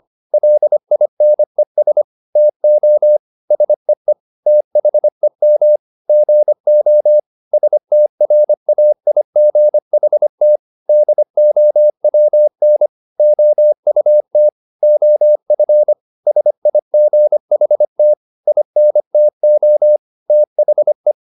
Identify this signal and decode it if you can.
Morse code, heard as a keyed tone, about 25 wpm, 600 Hz. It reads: LINES TO SEE THEM GO STRAIGHT DOWN OUT OF SIGHT INTO THE
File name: the old man and the sea part 1 25wpm